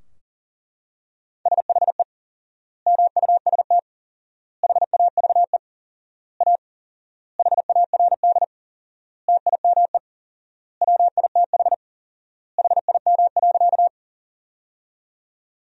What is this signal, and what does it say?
Morse code, heard as a keyed tone, about 40 wpm, 700 Hz. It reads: SHE MUST HAVE A HARD TIME WITH HIM.
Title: 4x Word-Spacing - Sentences from Top 300 Words - 40wpm